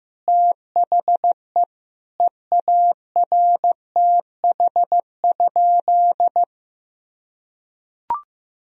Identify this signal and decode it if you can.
Morse code, heard as a keyed tone, about 16 wpm, 700 Hz. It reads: THE EARTH?
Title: Sentences from Top 1000 Words - Review Part 2 - 15wpm